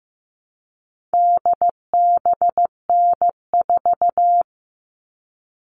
Morse code, heard as a keyed tone, about 15 wpm, 700 Hz. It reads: DBN4
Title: All Cut Numbers - 4-Digit 15wpm